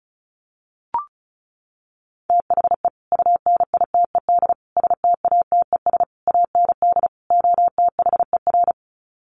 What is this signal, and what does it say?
Morse code, heard as a keyed tone, about 34 wpm, 700 Hz. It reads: THE UNITED STATES AND OTHER
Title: Sets of 5 Words Encore 35wpm